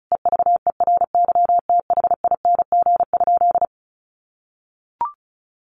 Morse code, heard as a keyed tone, about 34 wpm, 700 Hz. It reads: EVERYTHING?
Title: Sentences from Top 1100 Words 35wpm